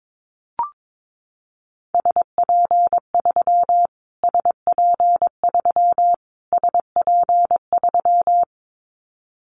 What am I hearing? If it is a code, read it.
Morse code, heard as a keyed tone, about 22 wpm, 700 Hz. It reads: SP3 SP3 SP3
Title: Callsign Prefixes for Top 25 DXCC - Repeated 3x - 22wpm